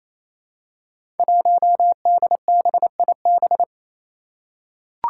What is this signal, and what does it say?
Morse code, heard as a keyed tone, about 28 wpm, 700 Hz. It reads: 1DBIB
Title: Five Challenging Letter-Number Combinations 28wpm